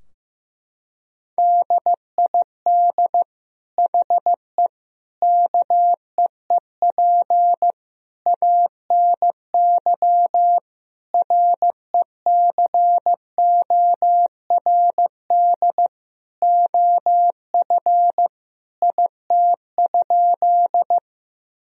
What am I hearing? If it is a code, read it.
Morse code, heard as a keyed tone, about 15 wpm, 700 Hz. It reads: DID HE KEEP ANY RECORD OF IT?
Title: Sentences from Top 1000 Words - Review Part 3 - 15wpm